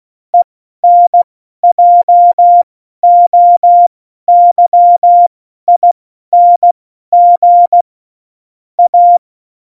Morse code, heard as a keyed tone, about 16 wpm, spaced as for 12 wpm, 700 Hz. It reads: ENJOYING A
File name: Morsekurs / Witze